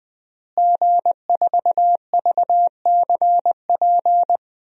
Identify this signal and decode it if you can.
Morse code, heard as a keyed tone, about 20 wpm, 700 Hz. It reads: G4VCP
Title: DXCC-100 Call Signs - 5 Characters - Repeated 2x - 20wpm